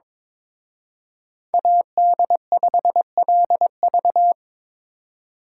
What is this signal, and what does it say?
Morse code, heard as a keyed tone, about 26 wpm, 700 Hz. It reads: AD5LV
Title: Speed Racing - USA Call Signs 22wpm